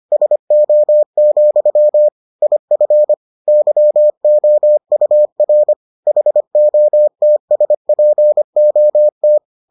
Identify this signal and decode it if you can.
Morse code, heard as a keyed tone, about 25 wpm, 600 Hz. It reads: SO, IF YOUR HOTSPOT